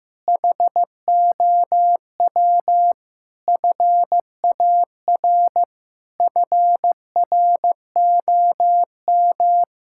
Morse code, heard as a keyed tone, about 16 wpm, 700 Hz. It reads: HOW FAR FROM
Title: Sentences from Top 1600 Words 15wpm